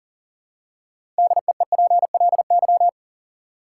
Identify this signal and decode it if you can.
Morse code, heard as a keyed tone, about 40 wpm, 700 Hz. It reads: DEEPLY
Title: Six Letter Words - 40wpm